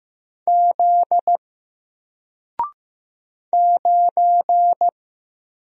Morse code, heard as a keyed tone, about 15 wpm, 700 Hz. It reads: Z 9
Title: Single Letter-Number 15wpm